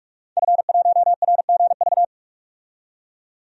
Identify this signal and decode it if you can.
Morse code, heard as a keyed tone, about 46 wpm, 700 Hz. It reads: F1RGV
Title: DXCC-100 Call Signs - 5 Characters - 45wpm